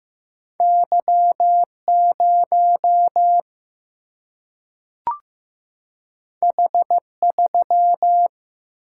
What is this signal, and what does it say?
Morse code, heard as a keyed tone, about 15 wpm, 700 Hz. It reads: Y0 H3
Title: Single-Letter Single-Number 15wpm